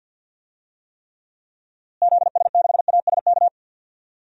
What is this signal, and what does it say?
Morse code, heard as a keyed tone, about 48 wpm, 700 Hz. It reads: ZS6ASK
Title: DXCC-100 Call Signs - 6 Characters - Repeated 3x - 50wpm